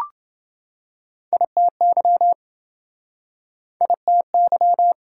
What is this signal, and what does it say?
Morse code, heard as a keyed tone, about 28 wpm, 700 Hz. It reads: ITY ITY